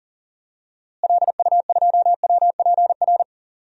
Morse code, heard as a keyed tone, about 37 wpm, 700 Hz. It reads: LU2WPR
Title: DXCC-100 Call Signs - 6 Characters - Repeated 2x - 40wpm